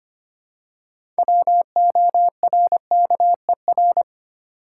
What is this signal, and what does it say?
Morse code, heard as a keyed tone, about 25 wpm, 700 Hz. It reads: WORKER
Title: Six Letter Words - 25wpm